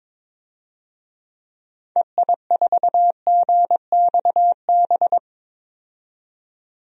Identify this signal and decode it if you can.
Morse code, heard as a keyed tone, about 22 wpm, 700 Hz. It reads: EI4GXB
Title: DXCC-100 Call Signs - 6 Characters - 22wpm